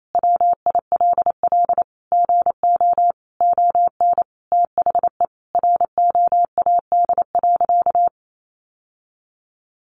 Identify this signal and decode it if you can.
Morse code, heard as a keyed tone, about 28 wpm, 700 Hz. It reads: WILL GO ON THE ROAD.
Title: Sentences from Top 1400 Words 28wpm